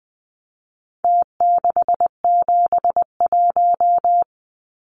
Morse code, heard as a keyed tone, about 19 wpm, 700 Hz. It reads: T671
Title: Numbers - Contest Style - 4-Digit 20wpm